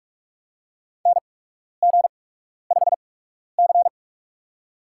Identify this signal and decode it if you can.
Morse code, heard as a keyed tone, about 47 wpm, spaced as for 12 wpm, 700 Hz. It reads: NG5C
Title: USA Call Signs - 8x Character Spacing - 45wpm